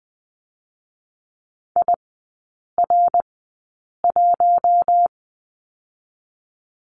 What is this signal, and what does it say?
Morse code, heard as a keyed tone, about 20 wpm, 700 Hz. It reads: I R 1
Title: 114 Lesson 19e - Letter w - Three Characters 20wpm